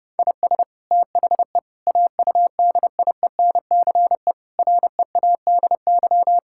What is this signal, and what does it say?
Morse code, heard as a keyed tone, about 30 wpm, 700 Hz. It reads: IS THE AUDIENCE READY
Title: Sentences from Top 1000 Words - Review Part 5 - 30wpm